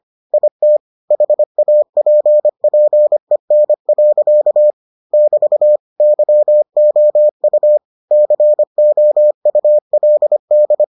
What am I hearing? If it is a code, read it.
Morse code, heard as a keyed tone, about 25 wpm, 600 Hz. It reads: IT HAPPEN. <BT> YOU COULD